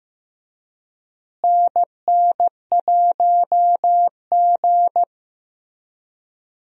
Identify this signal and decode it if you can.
Morse code, heard as a keyed tone, about 15 wpm, 700 Hz. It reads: NN1G
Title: USA Call Signs - 4 Characters - Repeated 3x - 15wpm